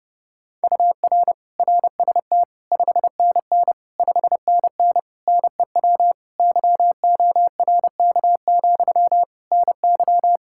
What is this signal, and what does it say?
Morse code, heard as a keyed tone, about 30 wpm, 700 Hz. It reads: UR RST 5NN 5NN NEW YORK, NY
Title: Practice QSO - Exchange 1 - Beginning - 30wpm